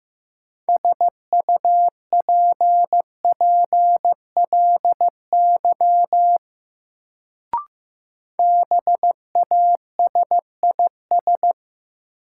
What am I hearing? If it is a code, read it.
Morse code, heard as a keyed tone, about 15 wpm, 700 Hz. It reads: SUPPLY BASIS
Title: Sentences from Top 1200 Words - New Word Drill Alt - 15wpm